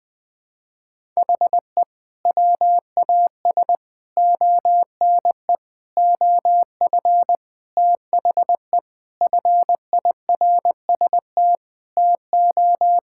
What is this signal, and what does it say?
Morse code, heard as a keyed tone, about 21 wpm, 700 Hz. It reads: HE WAS ONE OF THE FIRST TO
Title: Sentences with 3-Word Phrases - Spoken Beginning - 20wpm